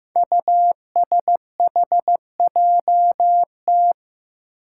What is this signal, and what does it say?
Morse code, heard as a keyed tone, about 15 wpm, 700 Hz. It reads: USHJT
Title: Five Challenging Letter-Number Combinations 15wpm